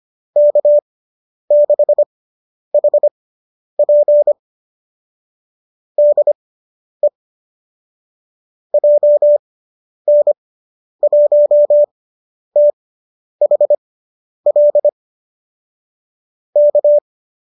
Morse code, heard as a keyed tone, about 25 wpm, spaced as for 10 wpm, 590 Hz. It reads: K6HP DE JN1THL K